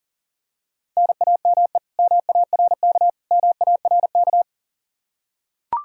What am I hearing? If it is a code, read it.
Morse code, heard as a keyed tone, about 37 wpm, 700 Hz. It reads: NAME MARK MARK
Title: Practice QSO - Exchange 1 - Beginning - 40wpm